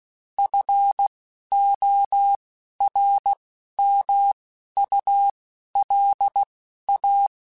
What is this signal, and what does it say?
Morse code, heard as a keyed tone, about 16 wpm, spaced as for 11 wpm, 800 Hz. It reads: FORMULA
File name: Technical Subjects / Technical Subjects-mp3s / XL @10